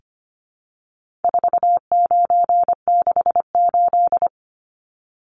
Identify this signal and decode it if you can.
Morse code, heard as a keyed tone, about 25 wpm, 700 Hz. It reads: 4968
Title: Numbers - 4-Digit 25wpm